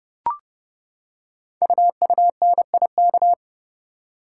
Morse code, heard as a keyed tone, about 30 wpm, 700 Hz. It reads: UUNIK
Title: Code Groups - 5 Letters - 30wpm